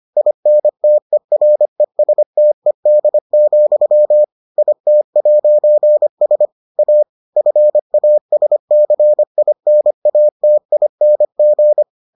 Morse code, heard as a keyed tone, about 25 wpm, 600 Hz. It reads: INTERESTED, IT'S A FASCINATING